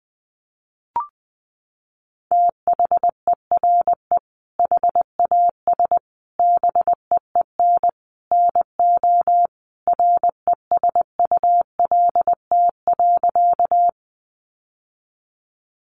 Morse code, heard as a keyed tone, about 20 wpm, 700 Hz. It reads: THERE HAS BEEN NO RESULT.
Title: Sentences from Top 700 Words 20wpm